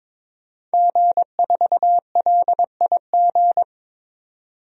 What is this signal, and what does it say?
Morse code, heard as a keyed tone, about 21 wpm, 700 Hz. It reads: G4LIG
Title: DXCC-100 Call Signs - 5 Characters - 22wpm